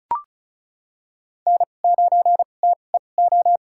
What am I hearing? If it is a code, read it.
Morse code, heard as a keyed tone, about 36 wpm, spaced as for 25 wpm, 700 Hz. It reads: N9TEO
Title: USA Call Signs - 2x Character Spacing - 35wpm